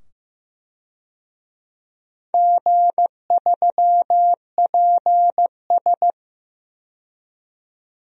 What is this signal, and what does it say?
Morse code, heard as a keyed tone, about 15 wpm, 700 Hz. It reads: G3PS
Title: DXCC-100 Call Signs - 4 Characters - 15wpm